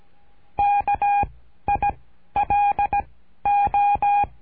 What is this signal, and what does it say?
Morse code, heard as a keyed tone, about 17 wpm, 810 Hz. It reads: K I L O
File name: cursos / CURSO DE TELEGRAFIA / Telegrafia - Trilha 13